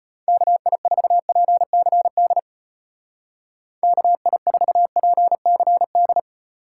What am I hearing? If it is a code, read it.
Morse code, heard as a keyed tone, about 33 wpm, 700 Hz. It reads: KI4PCD KI4PCD